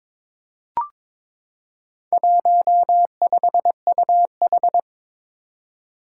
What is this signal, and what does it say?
Morse code, heard as a keyed tone, about 22 wpm, 700 Hz. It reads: 15UH
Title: Four Challenging Letter-Number Combinations 22wpm